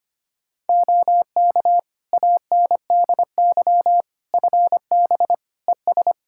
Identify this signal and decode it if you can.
Morse code, heard as a keyed tone, about 25 wpm, 700 Hz. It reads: OK ANDY FB ES
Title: K1USN SST QSO Practice 25wpm